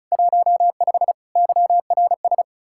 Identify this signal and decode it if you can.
Morse code, heard as a keyed tone, about 36 wpm, 700 Hz. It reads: 15 YRS
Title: Practice QSO - Exchange 3 - More Info - 35wpm